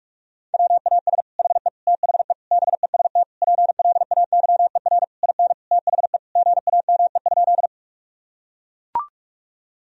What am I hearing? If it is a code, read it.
Morse code, heard as a keyed tone, about 41 wpm, 700 Hz. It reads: WAS HE THE BEST PLAYER IN THE GAME?
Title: Sentences from Top 1000 Words - Review Part 4 - 45wpm